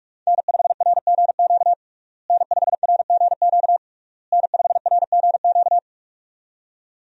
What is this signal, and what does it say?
Morse code, heard as a keyed tone, about 46 wpm, 700 Hz. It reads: N5RGQ N5RGQ N5RGQ
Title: USA Call Signs - 5 Characters - Repeated 3x - 45wpm